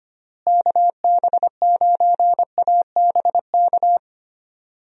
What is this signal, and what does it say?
Morse code, heard as a keyed tone, about 25 wpm, 700 Hz. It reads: KB9ABK
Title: USA Call Signs - 6 Characters - Repeated 3x - 25wpm